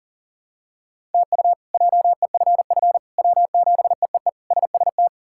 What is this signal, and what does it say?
Morse code, heard as a keyed tone, about 37 wpm, 700 Hz. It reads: TU JEFF W7EEE SST
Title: K1USN SST QSO Practice - 40wpm